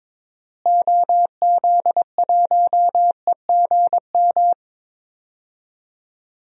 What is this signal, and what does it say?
Morse code, heard as a keyed tone, about 21 wpm, 700 Hz. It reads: OZ1EGM
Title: DXCC-100 Call Signs - 6 Characters - Repeated 3x - 22wpm